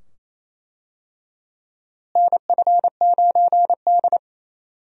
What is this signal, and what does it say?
Morse code, heard as a keyed tone, about 28 wpm, 700 Hz. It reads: NF9D
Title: USA Call Signs - 4 Characters - Repeated 2x - 28wpm